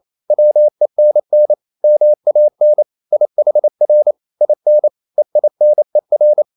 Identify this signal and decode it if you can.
Morse code, heard as a keyed tone, about 28 wpm, 600 Hz. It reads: WENN MAN IHR IN EINER